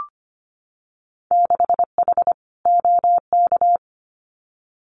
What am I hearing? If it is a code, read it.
Morse code, heard as a keyed tone, about 25 wpm, 700 Hz. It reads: 6H OK
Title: Winter Field Day Exchanges Encore 25wpm